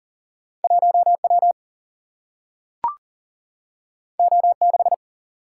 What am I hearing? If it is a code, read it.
Morse code, heard as a keyed tone, about 40 wpm, 700 Hz. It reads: 1 W O 6
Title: Single-Letter Single-Number 40wpm